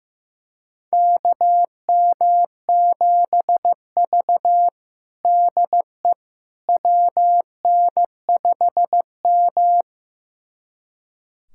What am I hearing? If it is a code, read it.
Morse code, heard as a keyed tone, about 15 wpm, 700 Hz. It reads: KM7V DE WN5M
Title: USA Callsign DE Callsign - 4 Characters - 15wpm